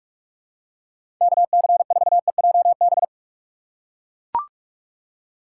Morse code, heard as a keyed tone, about 44 wpm, 700 Hz. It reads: KC4EJB
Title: USA Call Signs - 6x Character Spacing - 45wpm